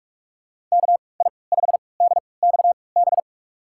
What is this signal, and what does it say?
Morse code, heard as a keyed tone, about 46 wpm, spaced as for 25 wpm, 700 Hz. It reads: KI5DXB
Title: USA Call Signs - 3x Character Spacing - 45wpm